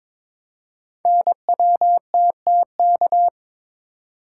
Morse code, heard as a keyed tone, about 22 wpm, 700 Hz. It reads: NWTTK
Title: Code Groups - 5 Letters - 22wpm